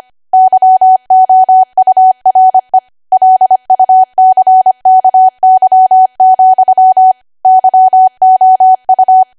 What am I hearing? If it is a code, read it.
Morse code, heard as a keyed tone, about 25 wpm, 750 Hz. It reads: YOURE LUCKY, YOU
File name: cwa / W1AW-Level3 / 25WPM